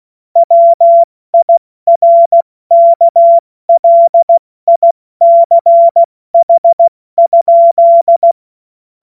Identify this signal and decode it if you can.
Morse code, heard as a keyed tone, about 16 wpm, 680 Hz. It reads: WIRKLICH?
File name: Morsekurs / Witze / Witz 16. Mai 2023 70 F80